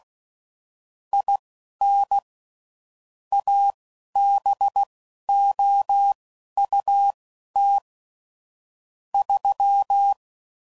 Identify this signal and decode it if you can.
Morse code, heard as a keyed tone, about 16 wpm, spaced as for 11 wpm, 800 Hz. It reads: IN ABOUT 3
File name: CW Code Groups @15WPM / QSOs / WD0BGP Spaced